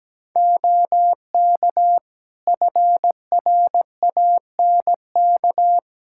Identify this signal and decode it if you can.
Morse code, heard as a keyed tone, about 17 wpm, 700 Hz. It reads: OK FRANK
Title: Practice QSO - Exchange 4 - Ending - 17wpm